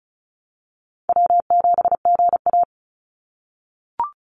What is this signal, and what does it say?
Morse code, heard as a keyed tone, about 34 wpm, 700 Hz. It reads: W7GA
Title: NA SOTA Chaser Callsigns 35wpm